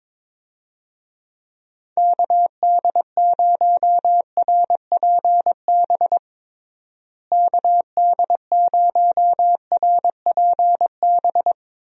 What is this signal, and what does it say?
Morse code, heard as a keyed tone, about 22 wpm, 700 Hz. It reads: KD0RPB KD0RPB
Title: USA Call Signs - 6 Characters - Repeated 3x - 22wpm